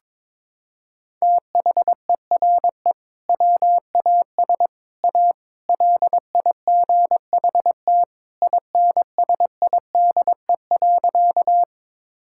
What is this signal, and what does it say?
Morse code, heard as a keyed tone, about 21 wpm, 700 Hz. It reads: THERE WAS A LIGHT INSIDE.
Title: Sentences from Top 1000 Words - Review Part 6 - 22wpm